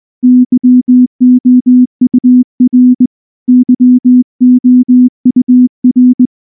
Morse code, heard as a keyed tone, about 19 wpm, 250 Hz. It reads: YOUR YOUR
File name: morse-training-20WPM